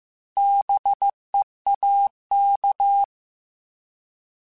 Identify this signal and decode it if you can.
Morse code, heard as a keyed tone, about 15 wpm, 800 Hz. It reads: BEAK
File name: Alphabetical / Alphabetical mp3s / Bs @13